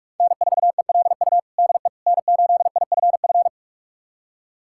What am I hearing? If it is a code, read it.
Morse code, heard as a keyed tone, about 41 wpm, 700 Hz. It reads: N4ELU DE N8IFF
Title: USA Callsign DE Callsign - 5 Characters - 45wpm